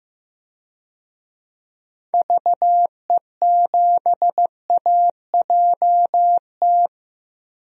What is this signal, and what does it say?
Morse code, heard as a keyed tone, about 15 wpm, 700 Hz. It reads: VE7AJT
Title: DXCC-100 Call Signs - 6 Characters - Repeated 2x - 15wpm